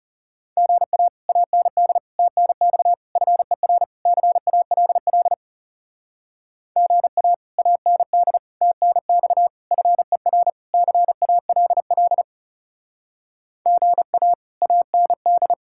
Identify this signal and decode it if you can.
Morse code, heard as a keyed tone, about 36 wpm, 700 Hz. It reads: GA AND TNX FER CALL GA AND TNX FER CALL GA AND
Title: Speed-Racing Common QSO Phrases - 30wpm